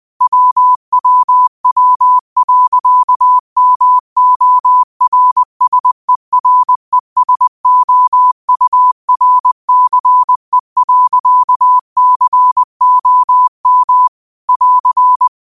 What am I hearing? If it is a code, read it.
Morse code, heard as a keyed tone, about 20 wpm, 1000 Hz. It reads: WWW.MORSERESOURCE.COM <AR>
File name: qotdfiles / 20wpm